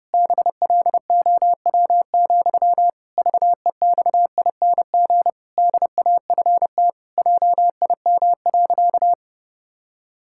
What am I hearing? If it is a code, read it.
Morse code, heard as a keyed tone, about 28 wpm, 700 Hz. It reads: BLOW, VEXING DAFT JIM.
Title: Pangram Challenge v2 - 30wpm